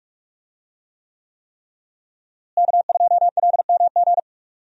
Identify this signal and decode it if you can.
Morse code, heard as a keyed tone, about 45 wpm, 700 Hz. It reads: K2LMG
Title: USA Call Signs - 5 Characters - Repeated 3x - 45wpm